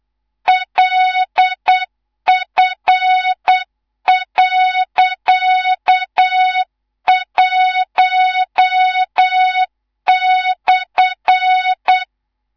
Morse code, heard as a keyed tone, about 8 wpm, 3700 Hz, with noise. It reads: LF.1/
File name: Morze-HA2MN / hang / 19 GYAKORLAT